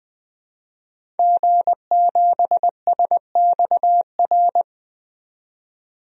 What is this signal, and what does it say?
Morse code, heard as a keyed tone, about 21 wpm, 700 Hz. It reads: G7SXR
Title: DXCC-100 Call Signs - 5 Characters - 20wpm